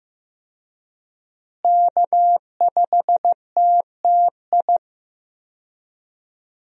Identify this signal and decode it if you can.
Morse code, heard as a keyed tone, about 15 wpm, 700 Hz. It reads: K5TTI
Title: USA Call Signs - 5 Characters - Repeated 3x - 15wpm